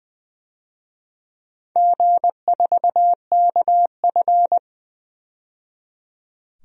Morse code, heard as a keyed tone, about 20 wpm, 700 Hz. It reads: G4KF
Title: DXCC-100 Call Signs - 4 Characters - 20wpm